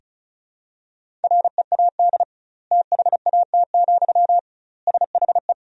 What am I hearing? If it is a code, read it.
Morse code, heard as a keyed tone, about 36 wpm, 700 Hz. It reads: READ THAT, SHE
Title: 2x Word-Spacing - Sentences from Top 200 Words - 35wpm